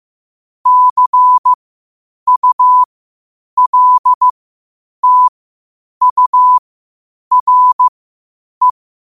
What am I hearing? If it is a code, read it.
Morse code, heard as a keyed tone, about 15 wpm, spaced as for 8 wpm, 1000 Hz. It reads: CULTURE
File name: qotdfiles / 8wpm